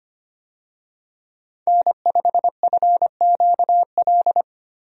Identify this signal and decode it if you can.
Morse code, heard as a keyed tone, about 25 wpm, 700 Hz. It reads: N5FQL
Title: USA Call Signs - 5 Characters - Repeated 2x - 25wpm